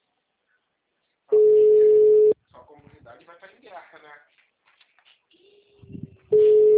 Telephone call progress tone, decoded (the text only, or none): ringback tone